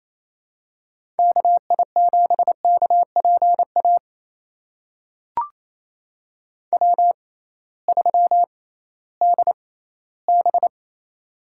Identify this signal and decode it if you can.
Morse code, heard as a keyed tone, about 28 wpm, spaced as for 16 wpm, 700 Hz. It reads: KI7KPA W3DB
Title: USA Call Signs - 6x Character Spacing - 28wpm